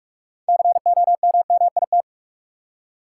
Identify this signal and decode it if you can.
Morse code, heard as a keyed tone, about 43 wpm, 700 Hz. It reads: COMMIT
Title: Six Letter Words - 45wpm